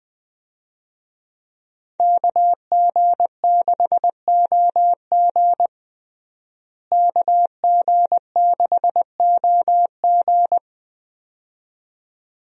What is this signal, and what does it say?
Morse code, heard as a keyed tone, about 21 wpm, 700 Hz. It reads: KG6OG KG6OG
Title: USA Call Signs - 5 Characters - Repeated 2x - 20wpm